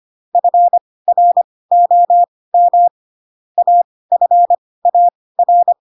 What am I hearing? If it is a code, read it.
Morse code, heard as a keyed tone, about 25 wpm, spaced as for 18 wpm, 700 Hz. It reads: FROM AFAR
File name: Pinocchio ch3 12 25 ik1hge